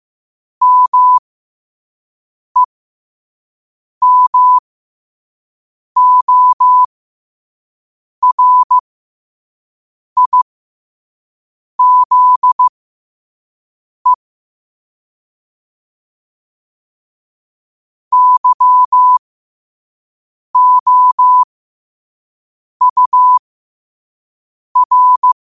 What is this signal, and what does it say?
Morse code, heard as a keyed tone, about 15 wpm, spaced as for 5 wpm, 1000 Hz. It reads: MEMORIZE YOUR
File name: qotdfiles / 5wpm